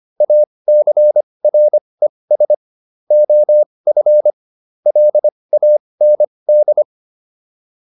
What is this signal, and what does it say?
Morse code, heard as a keyed tone, about 25 wpm, spaced as for 20 wpm, 600 Hz. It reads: ACRES OF LAND